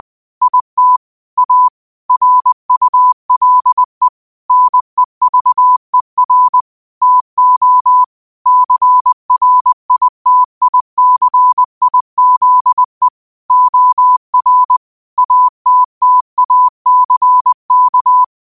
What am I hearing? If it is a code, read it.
Morse code, heard as a keyed tone, about 20 wpm, 1000 Hz. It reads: IT A RULE NEVER TO CRITICIZE OR ATTACK